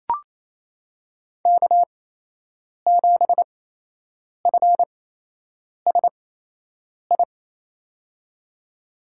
Morse code, heard as a keyed tone, about 27 wpm, spaced as for 8 wpm, 700 Hz. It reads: K7FSI
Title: USA Call Signs - 5 Characters - 8x Character Spacing - 28wpm